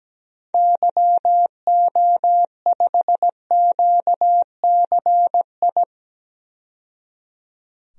Morse code, heard as a keyed tone, about 17 wpm, 700 Hz. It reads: YO5QCI
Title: DXCC-100 Call Signs - 6 Characters - 17wpm